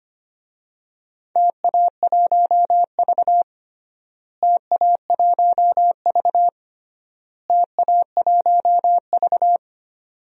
Morse code, heard as a keyed tone, about 25 wpm, 700 Hz. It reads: TA1V TA1V TA1V
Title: DXCC-100 Call Signs - 4 Characters - Repeated 3x - 25wpm